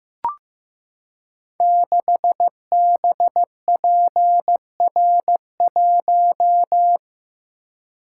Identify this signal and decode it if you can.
Morse code, heard as a keyed tone, about 15 wpm, 700 Hz. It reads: 6BPR1
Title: Code Groups - 5 Letter:Numbers - 15wpm